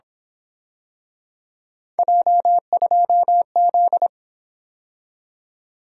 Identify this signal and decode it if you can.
Morse code, heard as a keyed tone, about 26 wpm, 700 Hz. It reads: J2Z
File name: Speed-Racing - Three Letter-Number Combinations - 22wpm